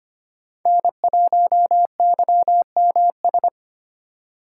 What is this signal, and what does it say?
Morse code, heard as a keyed tone, about 25 wpm, 700 Hz. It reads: N1YMS
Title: USA Call Signs - 5 Characters - 3x Character Spacing - 25wpm